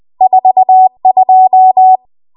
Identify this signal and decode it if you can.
Morse code, heard as a keyed tone, about 20 wpm, 750 Hz. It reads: 42
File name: Archive / 20 WPM / 20WPM